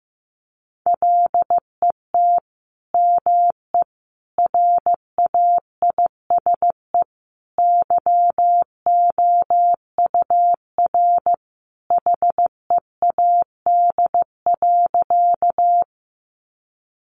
Morse code, heard as a keyed tone, about 14 wpm, 700 Hz. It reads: LET ME RAISE YOUR HEAD.
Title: Sentences from Top 1400 Words 15wpm